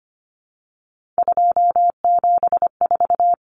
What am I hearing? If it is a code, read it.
Morse code, heard as a keyed tone, about 25 wpm, 700 Hz. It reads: 274
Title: Numbers - 3-Digit 25wpm